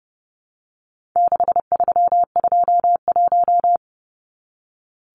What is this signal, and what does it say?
Morse code, heard as a keyed tone, about 30 wpm, 700 Hz. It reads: 6321
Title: Numbers - 4-Digit 30wpm